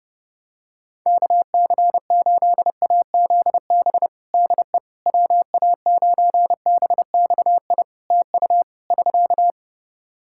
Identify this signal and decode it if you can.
Morse code, heard as a keyed tone, about 30 wpm, 700 Hz. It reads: KC8AZB DE WA9BXI TU <SK>
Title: K1USN SST QSO Practice 30wpm